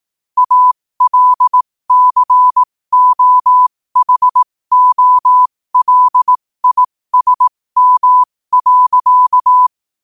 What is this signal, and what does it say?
Morse code, heard as a keyed tone, about 18 wpm, spaced as for 15 wpm, 1000 Hz. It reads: ALCOHOLISM.
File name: qotdfiles / 15wpm